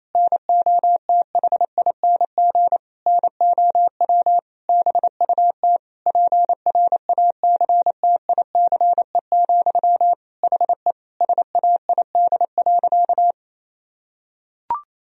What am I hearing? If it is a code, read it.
Morse code, heard as a keyed tone, about 28 wpm, 700 Hz. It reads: NOTHING NOW BUT PRACTICE, HE SAID.
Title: Sentences from Top 1000 Words - Review Part 4 - 28wpm